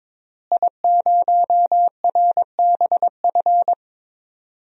Morse code, heard as a keyed tone, about 22 wpm, 700 Hz. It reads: I0RBF
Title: DXCC-100 Call Signs - 5 Characters - Repeated 2x - 22wpm